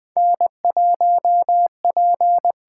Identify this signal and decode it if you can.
Morse code, heard as a keyed tone, about 20 wpm, 700 Hz. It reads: N1P
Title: USA Call Signs - 5 Characters - 8x Character Spacing - 20wpm